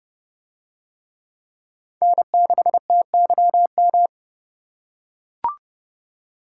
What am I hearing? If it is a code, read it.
Morse code, heard as a keyed tone, about 30 wpm, 700 Hz. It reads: N6TYM